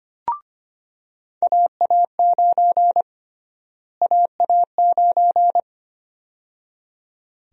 Morse code, heard as a keyed tone, about 25 wpm, 700 Hz. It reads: AA9 AA9
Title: USA Call Sign Prefixes - Repeated 2x - 25wpm